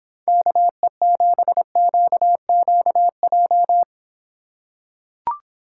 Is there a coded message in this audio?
Yes: Morse code from 0.3 to 3.8 s, heard as a keyed tone, about 26 wpm, 700 Hz. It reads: KE7QQJ